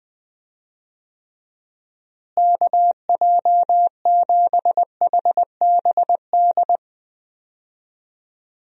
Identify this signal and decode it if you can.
Morse code, heard as a keyed tone, about 20 wpm, 700 Hz. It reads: KJ7HBD
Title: USA Call Signs - 6 Characters - Repeated 2x - 20wpm